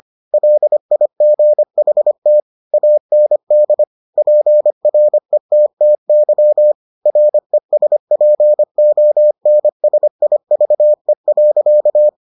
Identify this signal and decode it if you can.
Morse code, heard as a keyed tone, about 25 wpm, 600 Hz. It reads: LIGHT AND PRETTY RESPONSIVE.